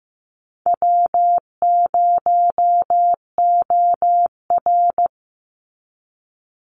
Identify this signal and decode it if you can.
Morse code, heard as a keyed tone, about 15 wpm, 700 Hz. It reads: W0OR
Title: Easy Call Signs Repeated 3x - 15wpm